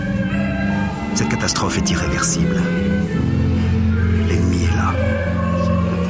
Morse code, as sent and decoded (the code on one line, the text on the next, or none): none